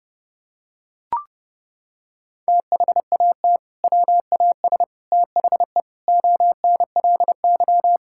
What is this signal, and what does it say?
Morse code, heard as a keyed tone, about 30 wpm, 700 Hz. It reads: THAT WAS THE ONLY
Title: Sentences from Top 1600 Words 30wpm